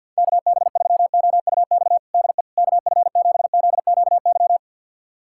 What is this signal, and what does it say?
Morse code, heard as a keyed tone, about 52 wpm, 700 Hz. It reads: KB3OUX DE KF7ZQY
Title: USA Callsign DE Callsign - 6 Characters - 50wpm